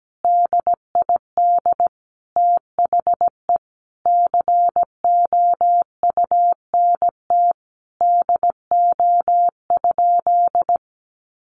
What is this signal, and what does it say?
Morse code, heard as a keyed tone, about 17 wpm, 700 Hz. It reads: DID THE COUNT DO?
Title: Sentences from Top 1500 Words 17wpm